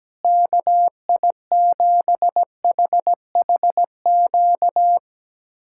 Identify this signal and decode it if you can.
Morse code, heard as a keyed tone, about 17 wpm, 700 Hz. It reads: KI7HHQ
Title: USA Callsign DE Callsign - 6 Characters - 17wpm